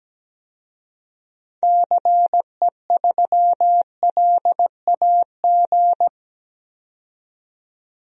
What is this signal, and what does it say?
Morse code, heard as a keyed tone, about 17 wpm, 700 Hz. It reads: CE3LAG
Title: DXCC-100 Call Signs - 6 Characters - 17wpm